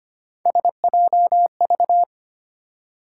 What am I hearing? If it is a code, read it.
Morse code, heard as a keyed tone, about 25 wpm, 700 Hz. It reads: SJV
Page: morse-code-ninja-practice-25wpm